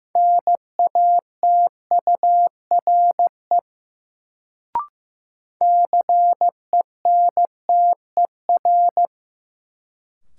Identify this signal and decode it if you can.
Morse code, heard as a keyed tone, about 15 wpm, 700 Hz. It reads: NATURE CENTER
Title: Sets of 3 Words - New Word Drill - 15wpm